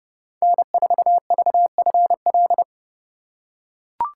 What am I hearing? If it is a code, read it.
Morse code, heard as a keyed tone, about 30 wpm, 700 Hz. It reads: N4VFL
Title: USA Call Signs - 5 Characters - 2x Character Spacing - 30wpm